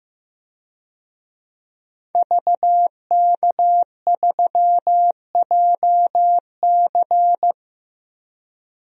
Morse code, heard as a keyed tone, about 15 wpm, 700 Hz. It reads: VK3JC
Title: DXCC-100 Call Signs - 5 Characters - Repeated 2x - 15wpm